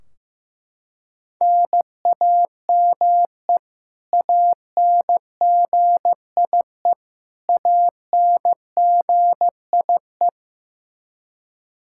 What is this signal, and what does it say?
Morse code, heard as a keyed tone, about 15 wpm, 700 Hz. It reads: NAME ANGIE ANGIE
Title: Practice QSO - Exchange 1 - Beginning - 15wpm